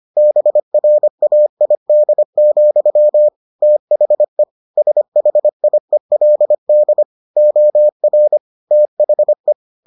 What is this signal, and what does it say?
Morse code, heard as a keyed tone, about 25 wpm, 600 Hz. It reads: BRAID, THE SHIELD OR THE